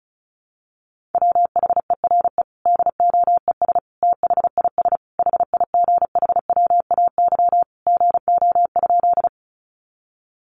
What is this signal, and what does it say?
Morse code, heard as a keyed tone, about 34 wpm, 700 Hz. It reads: WHERE DOES THIS HIGHWAY GO?
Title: Sentences from Top 1400 Words 35wpm